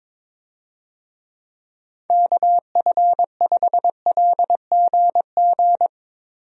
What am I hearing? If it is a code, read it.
Morse code, heard as a keyed tone, about 22 wpm, 700 Hz. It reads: KF5LGG
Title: USA Call Signs - 6 Characters - Repeated 2x - 22wpm